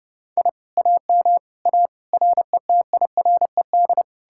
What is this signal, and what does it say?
Morse code, heard as a keyed tone, about 40 wpm, 700 Hz. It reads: I AM A RETIRED